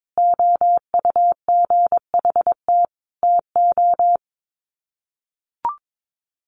Morse code, heard as a keyed tone, about 22 wpm, 700 Hz. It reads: OUGHT TO
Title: Sets of 5 Words Encore 22wpm